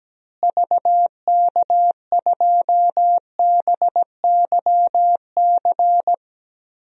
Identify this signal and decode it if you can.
Morse code, heard as a keyed tone, about 17 wpm, 700 Hz. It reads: VK2BYC
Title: DXCC-100 Call Signs - 6 Characters - Repeated 3x - 17wpm